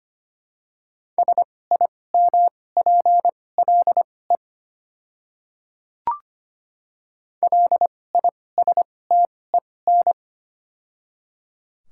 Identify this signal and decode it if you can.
Morse code, heard as a keyed tone, about 25 wpm, spaced as for 18 wpm, 700 Hz. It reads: SIMPLE LISTEN
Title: 2x Character Spacing - Top 500 Words - 25wpm